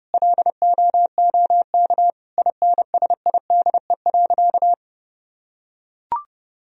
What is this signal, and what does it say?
Morse code, heard as a keyed tone, about 30 wpm, 700 Hz. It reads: LOOK INSIDE.
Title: Sentences from Top 1000 Words - Review Part 2 - 30wpm